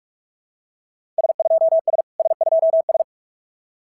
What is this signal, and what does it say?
Morse code, heard as a keyed tone, about 46 wpm, 690 Hz. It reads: S2S S2S
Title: SOTA QSO Practice 45wpm